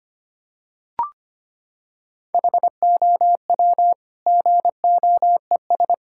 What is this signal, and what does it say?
Morse code, heard as a keyed tone, about 25 wpm, 700 Hz. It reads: HOW GOES
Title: Sentences from Top 1000 Words - Review Part 5 - 25wpm